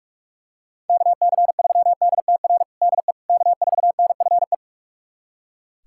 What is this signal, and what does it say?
Morse code, heard as a keyed tone, about 42 wpm, 700 Hz. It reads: KC3DTR DE K4NFE
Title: USA Callsign DE Callsign 45wpm